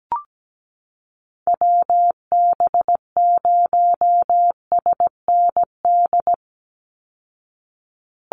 Morse code, heard as a keyed tone, about 17 wpm, 700 Hz. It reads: WB0SND
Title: Call Signs Repeated 3x - 17wpm